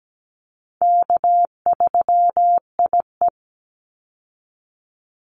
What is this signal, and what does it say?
Morse code, heard as a keyed tone, about 17 wpm, 700 Hz. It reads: K3IE
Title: Easy Call Signs Repeated 3x - 17wpm